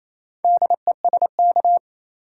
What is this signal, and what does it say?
Morse code, heard as a keyed tone, about 28 wpm, 700 Hz. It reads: DESK
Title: Four Letter Words - 28wpm